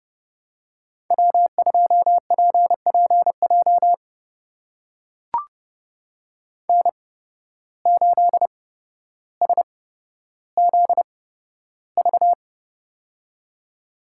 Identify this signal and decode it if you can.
Morse code, heard as a keyed tone, about 30 wpm, spaced as for 13 wpm, 700 Hz. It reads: W2PPJ N8SZV
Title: USA Call Signs - 5 Characters - 8x Character Spacing - 30wpm